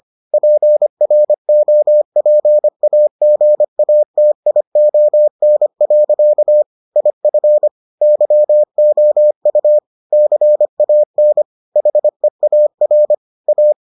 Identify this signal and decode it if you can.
Morse code, heard as a keyed tone, about 25 wpm, 600 Hz. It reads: PROPAGATION. IF YOU CAN HEAR A